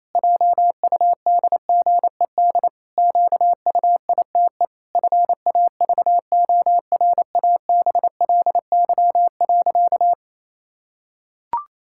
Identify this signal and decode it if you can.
Morse code, heard as a keyed tone, about 28 wpm, 700 Hz. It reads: JUDGED QUITE FAVORABLY.
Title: Pangram Challenge v2 - 28wpm